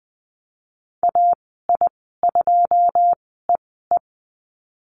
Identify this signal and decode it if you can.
Morse code, heard as a keyed tone, about 19 wpm, spaced as for 14 wpm, 700 Hz. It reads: AI2EE
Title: 140 Lesson 23g - Letter f - Call Signs 20wpm